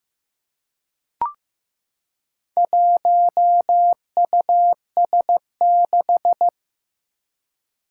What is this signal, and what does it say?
Morse code, heard as a keyed tone, about 15 wpm, 700 Hz. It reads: 1US6
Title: Four Challenging Letter-Number Combinations 15wpm